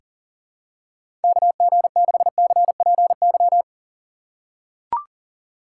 Morse code, heard as a keyed tone, about 37 wpm, 700 Hz. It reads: KG6CPY
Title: USA Call Signs - 6 Characters - 6x Character Spacing - 40wpm